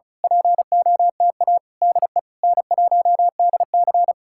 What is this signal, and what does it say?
Morse code, heard as a keyed tone, about 33 wpm, 700 Hz. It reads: POTA DE N1DC
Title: Speed-Racing Common QSO Phrases - 35wpm